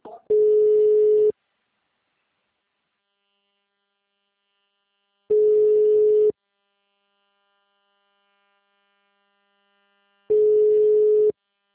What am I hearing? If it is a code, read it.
Ringback tone (Europe, 425 Hz).